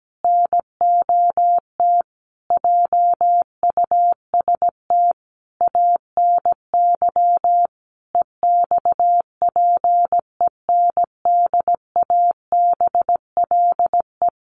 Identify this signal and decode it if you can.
Morse code, heard as a keyed tone, about 17 wpm, 700 Hz. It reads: NOT JUST ANY EXPENDABLE
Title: A Sound Of Thunder 17wpm